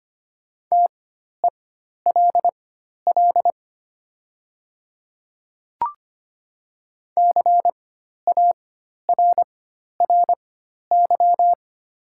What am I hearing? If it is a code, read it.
Morse code, heard as a keyed tone, about 25 wpm, spaced as for 12 wpm, 700 Hz. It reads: TELL CARRY
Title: 4x Character Spacing - Top 300 Words - 25wpm